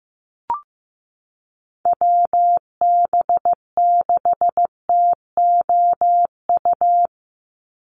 Morse code, heard as a keyed tone, about 15 wpm, 700 Hz. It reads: WB6TOU
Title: Call Signs Repeated 2x - 15wpm